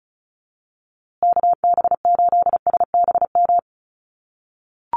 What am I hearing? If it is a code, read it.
Morse code, heard as a keyed tone, about 34 wpm, 700 Hz. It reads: KB8SBM
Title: USA Call Signs Repeated 2x 35wpm